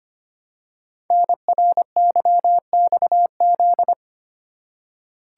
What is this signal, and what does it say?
Morse code, heard as a keyed tone, about 25 wpm, 700 Hz. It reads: NRYXZ
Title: Code Groups - 5 Letters - 25wpm